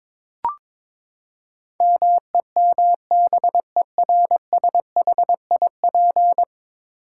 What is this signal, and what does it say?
Morse code, heard as a keyed tone, about 22 wpm, 700 Hz. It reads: MEMBERSHIP
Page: morse-code-ninja-practice-22wpm